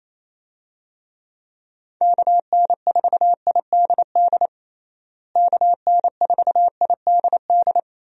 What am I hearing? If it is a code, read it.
Morse code, heard as a keyed tone, about 28 wpm, 700 Hz. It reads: KN4IDD KN4IDD
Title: USA Call Signs - 6 Characters - Repeated 3x - 28wpm